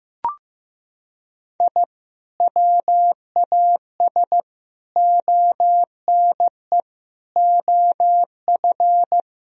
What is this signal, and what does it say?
Morse code, heard as a keyed tone, about 15 wpm, 700 Hz. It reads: I WAS ONE OF
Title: Sentences from Top 1000 Words - Review Part 1 - 15wpm